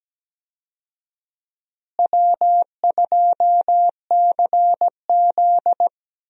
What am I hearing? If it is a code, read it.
Morse code, heard as a keyed tone, about 17 wpm, 700 Hz. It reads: W2CZ
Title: USA Call Signs - 4 Characters - Repeated 2x - 17wpm